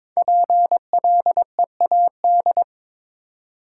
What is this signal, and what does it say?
Morse code, heard as a keyed tone, about 22 wpm, 700 Hz. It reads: PLEAD
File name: 5 Letter Words - First 3 Letters Given - 22wpm